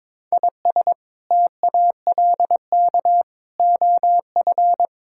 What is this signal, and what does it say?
Morse code, heard as a keyed tone, about 22 wpm, 700 Hz. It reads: IS TALK OF
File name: Build A Phrase - 5-Word Phrases - 22wpm